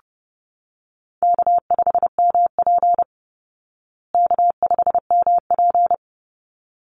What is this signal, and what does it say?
Morse code, heard as a keyed tone, about 30 wpm, 700 Hz. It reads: K5MP K5MP
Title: Easy Call Signs Repeated 2x - 30wpm